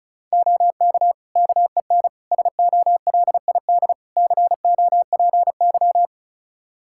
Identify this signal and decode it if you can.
Morse code, heard as a keyed tone, about 34 wpm, 700 Hz. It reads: OK KEN SOLID COPY
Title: Practice QSO - Exchange 3 - More Info - 35wpm